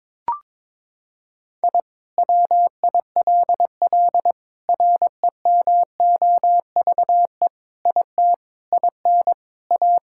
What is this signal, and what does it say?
Morse code, heard as a keyed tone, about 21 wpm, 700 Hz. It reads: I WILL REMOVE IT IN A
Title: Sentences from Top 1000 Words - Review Part 5 - 22wpm